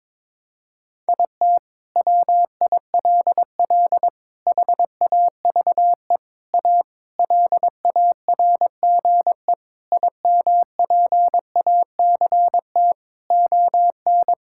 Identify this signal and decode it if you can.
Morse code, heard as a keyed tone, about 22 wpm, 700 Hz. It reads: IT WILL HAVE A LARGE IMPACT ON
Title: Sentences from Top 1000 Words - Review Part 6 - 22wpm